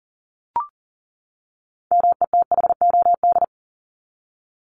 Morse code, heard as a keyed tone, about 40 wpm, 700 Hz. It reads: METHOD
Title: Top 501-750 Words 40wpm